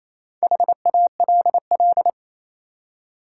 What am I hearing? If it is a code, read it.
Morse code, heard as a keyed tone, about 28 wpm, 700 Hz. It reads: HALL
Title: Four Letter Words - 28wpm